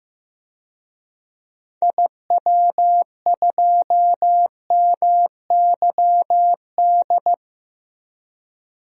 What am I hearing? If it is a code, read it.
Morse code, heard as a keyed tone, about 15 wpm, 700 Hz. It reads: IW2MYD
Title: DXCC-100 Call Signs - 6 Characters - Repeated 2x - 15wpm